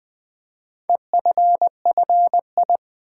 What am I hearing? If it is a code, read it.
Morse code, heard as a keyed tone, about 20 wpm, 700 Hz. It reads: EFFI
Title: Build A Word - 4 Letters At A Time - 11-Letter Words - 20wpm